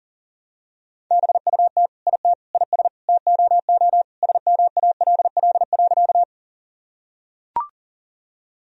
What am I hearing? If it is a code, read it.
Morse code, heard as a keyed tone, about 40 wpm, 700 Hz. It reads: BUT IT IS TOO SMALL.
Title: Sentences from Top 1000 Words - Review Part 1 - 40wpm